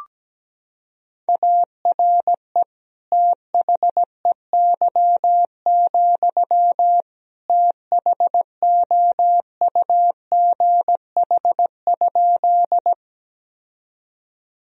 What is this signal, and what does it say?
Morse code, heard as a keyed tone, about 17 wpm, 700 Hz. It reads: ARE THEY, THOUGH?
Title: Sentences from Top 1000 Words - Review Part 2 - 17wpm